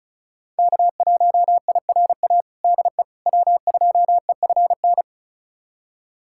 Morse code, heard as a keyed tone, about 34 wpm, 700 Hz. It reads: K1IRA DE W2EFN
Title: USA Callsign DE Callsign - 5 Characters - 35wpm